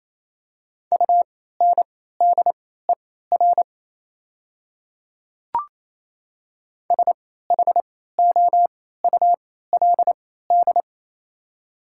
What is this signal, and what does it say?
Morse code, heard as a keyed tone, about 28 wpm, spaced as for 16 wpm, 700 Hz. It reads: UNDER SHOULD
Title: 3x Character Spacing - Top 300 Words - 28wpm